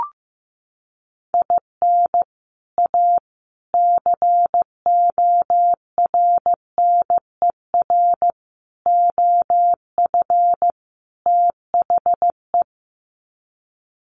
Morse code, heard as a keyed tone, about 15 wpm, 700 Hz. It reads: IN A CORNER OF THE
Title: Sets of 5 Words Encore2 15wpm